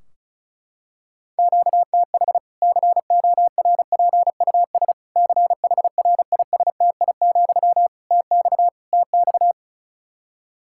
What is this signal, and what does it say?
Morse code, heard as a keyed tone, about 34 wpm, 700 Hz. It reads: QTH CORPUS CHRISTI, TX TX
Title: K1USN SST QSO Practice 35wpm